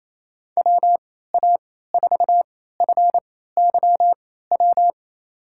Morse code, heard as a keyed tone, about 28 wpm, spaced as for 16 wpm, 700 Hz. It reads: WA4FYW
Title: USA Call Signs - 3x Character Spacing - 28wpm